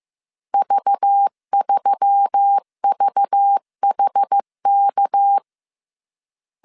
Morse code, heard as a keyed tone, about 15 wpm, 800 Hz. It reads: V3VHK